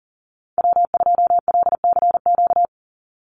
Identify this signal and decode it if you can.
Morse code, heard as a keyed tone, about 40 wpm, 700 Hz. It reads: W2LCQ
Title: Call Signs - 40wpm